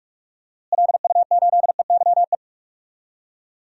Morse code, heard as a keyed tone, about 46 wpm, 700 Hz. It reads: LU8EYE
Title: DXCC-100 Call Signs - 6 Characters - Repeated 2x - 45wpm